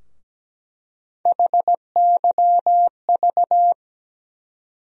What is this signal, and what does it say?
Morse code, heard as a keyed tone, about 17 wpm, 700 Hz. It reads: HYV
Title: USA Call Sign Postfixes - 3 Letters 17wpm